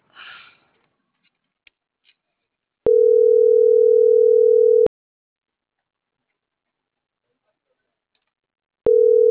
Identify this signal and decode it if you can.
Ringback tone (North America), 440 + 480 Hz.